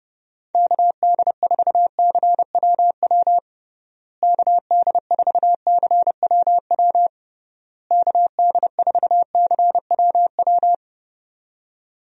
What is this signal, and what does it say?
Morse code, heard as a keyed tone, about 32 wpm, 700 Hz. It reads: KD4CWW KD4CWW KD4CWW
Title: USA Call Signs - 6 Characters - Repeated 3x - 30wpm